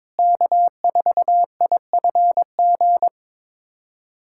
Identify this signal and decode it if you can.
Morse code, heard as a keyed tone, about 22 wpm, 700 Hz. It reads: K4IFG
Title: USA Callsign DE Callsign - 5 Characters - 22wpm